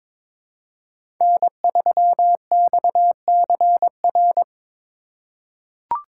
Morse code, heard as a keyed tone, about 22 wpm, 700 Hz. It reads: N3XCR
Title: USA Call Signs - 5 Characters - 3x Character Spacing - 22wpm